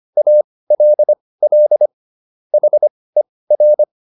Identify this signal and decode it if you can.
Morse code, heard as a keyed tone, about 25 wpm, spaced as for 18 wpm, 600 Hz. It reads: ALL HER